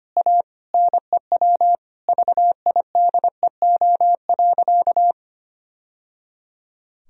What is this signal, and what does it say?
Morse code, heard as a keyed tone, about 25 wpm, 700 Hz. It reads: A NEW VIDEO.
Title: Sentences from Top 1000 Words - Review Part 4 - 25wpm